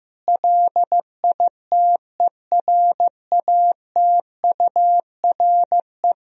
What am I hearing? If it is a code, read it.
Morse code, heard as a keyed tone, about 15 wpm, 700 Hz. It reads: LITERATURE
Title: Sentences from Top 1000 Words - Review Part 3 - 15wpm